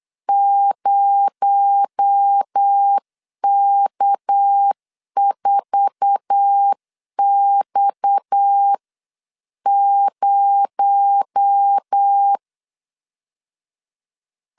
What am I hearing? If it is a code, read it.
Morse code, heard as a keyed tone, about 8 wpm, 800 Hz. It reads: 0K4X0
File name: CW / DET / 1 9 1500C S A 8wpm 5637